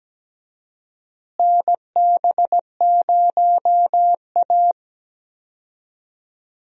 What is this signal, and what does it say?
Morse code, heard as a keyed tone, about 17 wpm, 700 Hz. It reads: NB0A
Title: USA Call Signs - 4 Characters - Repeated 3x - 17wpm